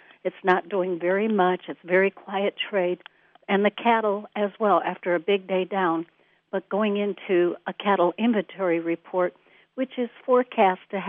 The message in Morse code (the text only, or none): none